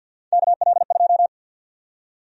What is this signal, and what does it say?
Morse code, heard as a keyed tone, about 50 wpm, 700 Hz. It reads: KL2
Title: USA Call Sign Prefixes - 50wpm